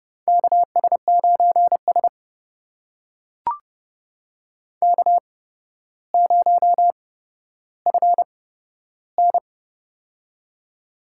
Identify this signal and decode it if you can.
Morse code, heard as a keyed tone, about 30 wpm, spaced as for 13 wpm, 700 Hz. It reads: KS9S K0FN
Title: USA Call Signs - 4 Characters - 8x Character Spacing - 30wpm